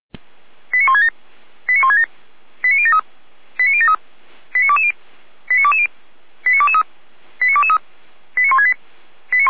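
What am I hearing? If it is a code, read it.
Five-tone selective call (ZVEI), heard as quick runs of five tones, tones of about 70 ms. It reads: 89167 89167 89073 89073 89220 89220 89223 89223 89168
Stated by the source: ZVEI five-tone call 89011 received on SDR.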